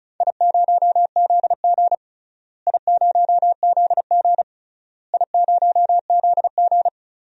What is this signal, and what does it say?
Morse code, heard as a keyed tone, about 36 wpm, 700 Hz. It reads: I0ZG I0ZG I0ZG
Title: DXCC-100 Call Signs - 4 Characters - Repeated 3x - 35wpm